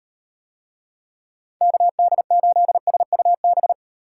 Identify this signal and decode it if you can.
Morse code, heard as a keyed tone, about 35 wpm, 700 Hz. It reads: KD8SUB